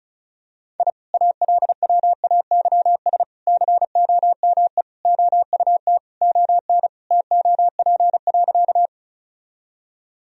Morse code, heard as a keyed tone, about 34 wpm, 700 Hz. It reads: I ALWAYS COME OUT ON TOP.
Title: Sentences from Top 1000 Words - Review Part 2 - 35wpm